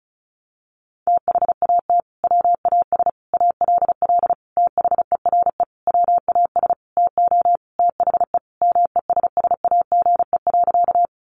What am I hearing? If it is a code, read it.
Morse code, heard as a keyed tone, about 34 wpm, 700 Hz. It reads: THAT WAS ALL THERE WAS TO THE MESSAGE.
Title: Sentences from Top 1100 Words 35wpm